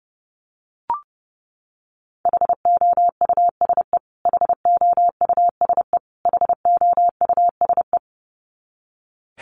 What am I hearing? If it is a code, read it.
Morse code, heard as a keyed tone, about 30 wpm, 700 Hz. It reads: HOUSE HOUSE HOUSE
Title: Top 500 Words Repeated 30wpm